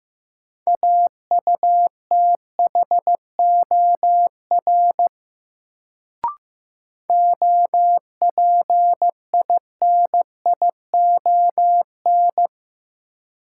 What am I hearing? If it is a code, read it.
Morse code, heard as a keyed tone, about 15 wpm, 700 Hz. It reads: AUTHOR OPINION
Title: Sentences from Top 1200 Words - New Word Drill Alt - 15wpm